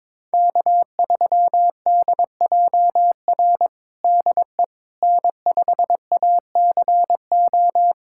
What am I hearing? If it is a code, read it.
Morse code, heard as a keyed tone, about 22 wpm, 700 Hz. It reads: K3DJR DE N5ACO
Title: USA Callsign DE Callsign - 5 Characters - 22wpm